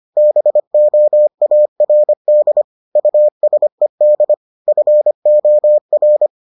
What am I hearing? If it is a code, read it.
Morse code, heard as a keyed tone, about 25 wpm, 600 Hz. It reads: BOARD USED FOR